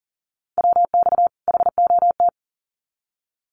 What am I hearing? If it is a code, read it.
Morse code, heard as a keyed tone, about 40 wpm, 700 Hz. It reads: WX HOT
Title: Common QSO Phrases 40wpm